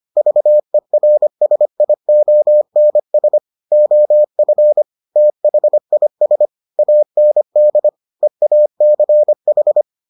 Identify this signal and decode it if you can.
Morse code, heard as a keyed tone, about 25 wpm, 600 Hz. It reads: VERSIONS OF THIS AND EACH